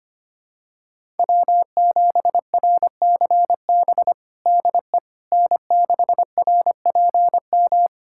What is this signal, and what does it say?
Morse code, heard as a keyed tone, about 25 wpm, 700 Hz. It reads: W7RCB DE N6RPM
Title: USA Callsign DE Callsign - 5 Characters - 25wpm